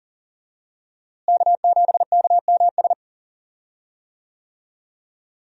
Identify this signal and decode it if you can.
Morse code, heard as a keyed tone, about 40 wpm, 700 Hz. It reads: K7KMS
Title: USA Call Signs - 5 Characters - Repeated 3x - 40wpm